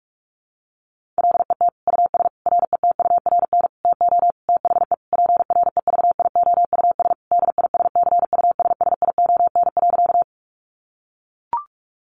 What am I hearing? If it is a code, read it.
Morse code, heard as a keyed tone, about 44 wpm, 700 Hz. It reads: LET US RETURN TO THE PREVIOUS DISCUSSION.
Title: Sentences from Top 1500 Words 45wpm